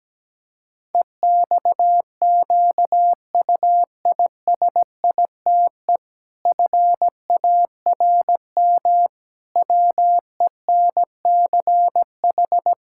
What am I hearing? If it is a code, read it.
Morse code, heard as a keyed tone, about 17 wpm, 700 Hz. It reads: EXQUISITE FARM WENCH
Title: Pangram Challenge v2 - 17wpm